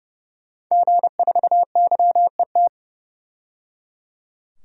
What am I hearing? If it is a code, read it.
Morse code, heard as a keyed tone, about 30 wpm, 700 Hz. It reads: G4YET
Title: DXCC-100 Call Signs - 5 Characters - 30wpm